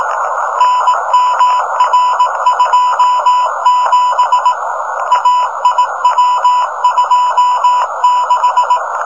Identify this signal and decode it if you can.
Morse code, heard as a keyed tone, about 18 wpm, 1000 Hz, with noise. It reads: NMR27 AIW26
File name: rs-15 cw telemetry